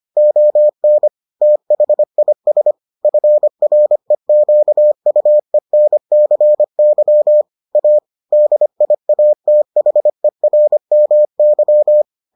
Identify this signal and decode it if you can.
Morse code, heard as a keyed tone, about 25 wpm, 600 Hz. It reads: ON THIS FREQUENCY A DIATHERMY